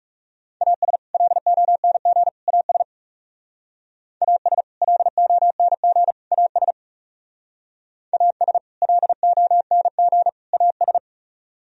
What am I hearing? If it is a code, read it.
Morse code, heard as a keyed tone, about 40 wpm, 700 Hz. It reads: AS LONG AS AS LONG AS AS LONG AS